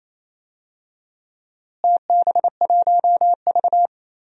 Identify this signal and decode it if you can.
Morse code, heard as a keyed tone, about 28 wpm, 700 Hz. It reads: TB1V
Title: DXCC-100 Call Signs - 4 Characters - Repeated 2x - 28wpm